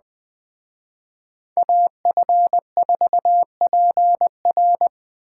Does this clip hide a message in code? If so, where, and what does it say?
Yes: Morse code from 1.6 to 4.9 s, heard as a keyed tone, about 20 wpm, 700 Hz. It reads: AF4PR